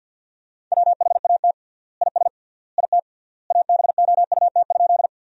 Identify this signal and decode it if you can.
Morse code, heard as a keyed tone, about 50 wpm, 700 Hz. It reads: WHAT IS IT ABOUT?
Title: 3x Word-Spacing - Sentences from Top 100 Words - 50wpm